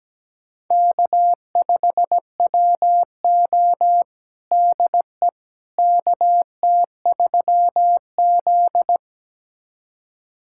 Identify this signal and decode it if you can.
Morse code, heard as a keyed tone, about 17 wpm, 700 Hz. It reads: K5WO DE KT3Z
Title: USA Callsign DE Callsign - 4 Characters - 17wpm